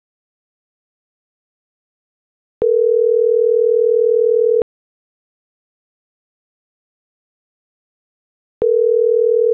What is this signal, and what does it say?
Ringback tone (North America), 440 + 480 Hz.